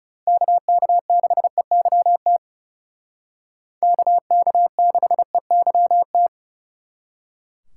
Morse code, heard as a keyed tone, about 37 wpm, 700 Hz. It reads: KK6EYT KK6EYT
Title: Speed Racing - USA 2x3 Call Signs 30wpm